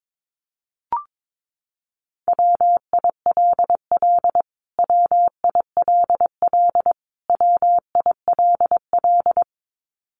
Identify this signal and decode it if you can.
Morse code, heard as a keyed tone, about 22 wpm, 700 Hz. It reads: WILL WILL WILL
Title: Top 500 Words Repeated 22wpm